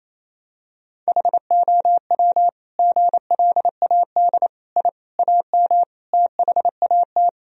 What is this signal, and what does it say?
Morse code, heard as a keyed tone, about 28 wpm, 700 Hz. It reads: HOW GLAD I AM THAT
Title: Sentences from Top 1000 Words - Review Part 2 - 28wpm